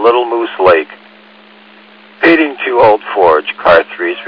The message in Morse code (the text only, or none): none